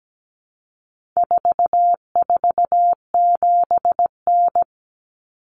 Morse code, heard as a keyed tone, about 17 wpm, 700 Hz. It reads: 447N
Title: Common Cut Numbers - 4-Digit 17wpm